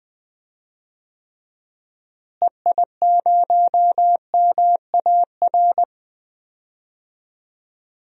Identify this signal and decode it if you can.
Morse code, heard as a keyed tone, about 20 wpm, 700 Hz. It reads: EI0MAR
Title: DXCC-100 Call Signs - 6 Characters - Repeated 3x - 20wpm